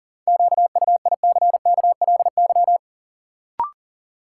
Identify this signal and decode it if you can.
Morse code, heard as a keyed tone, about 40 wpm, 700 Hz. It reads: QUICKLY
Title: Sentences from Top 700 Words - New Word Drill - 40wpm